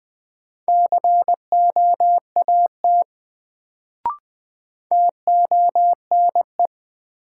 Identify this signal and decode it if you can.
Morse code, heard as a keyed tone, about 20 wpm, 700 Hz. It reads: COAT TONE
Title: Sentences from Top 1600 Words - New Word Drill Alt - 20wpm